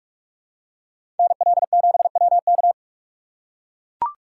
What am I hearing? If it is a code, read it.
Morse code, heard as a keyed tone, about 44 wpm, 700 Hz. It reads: NL7WK